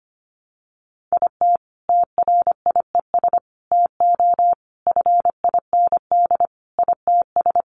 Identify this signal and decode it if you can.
Morse code, heard as a keyed tone, about 25 wpm, 700 Hz. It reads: IT TRIES TO FIND ITS
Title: Additional Easy Phrases 25wpm